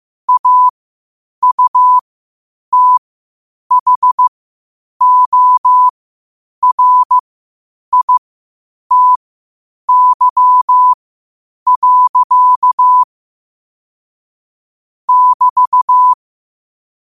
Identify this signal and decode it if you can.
Morse code, heard as a keyed tone, about 15 wpm, spaced as for 8 wpm, 1000 Hz. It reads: AUTHORITY. <BT>
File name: qotdfiles / 8wpm